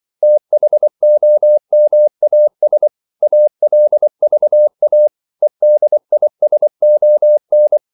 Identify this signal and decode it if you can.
Morse code, heard as a keyed tone, about 24 wpm, 600 Hz. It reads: THOMAS ALVA EDISON
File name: CW / Practice / EnglishQuotations0224